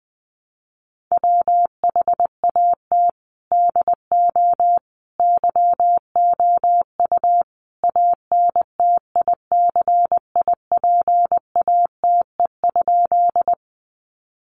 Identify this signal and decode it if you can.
Morse code, heard as a keyed tone, about 20 wpm, 700 Hz. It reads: WHAT DO YOU ANTICIPATE?
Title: Sentences from Top 1500 Words 20wpm